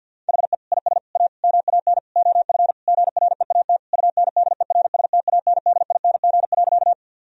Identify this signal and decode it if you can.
Morse code, heard as a keyed tone, about 50 wpm, 700 Hz. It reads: HE IS A MAN OF GREAT UNDERSTANDING.
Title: Sentences from Top 1000 Words - Review Part 3 - 50wpm